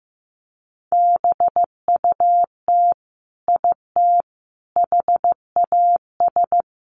Morse code, heard as a keyed tone, about 15 wpm, 700 Hz. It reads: BUT IT HAS
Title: Sentences from Top 100 Words 15wpm